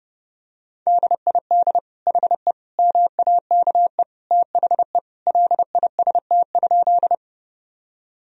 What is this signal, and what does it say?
Morse code, heard as a keyed tone, about 30 wpm, 700 Hz. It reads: DID HE MAKE THE LIST?
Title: Sentences from Top 1000 Words - Review Part 2 - 30wpm